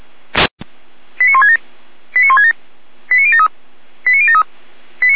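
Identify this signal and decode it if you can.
Five-tone selective call (ZVEI), heard as quick runs of five tones, tones of about 70 ms. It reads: 89167 89167 89073 89073
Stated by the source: ZVEI five-tone call received on an SDR.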